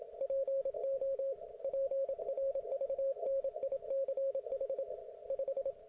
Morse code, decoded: PO PRVNICH 5

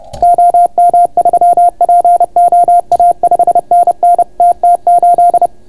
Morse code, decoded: OM3POA 5NN TT8

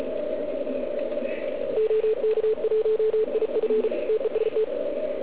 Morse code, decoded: OK1IF TEST

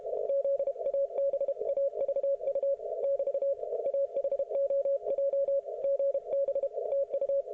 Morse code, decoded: ZABAVU <BT> AHOJ GB TU

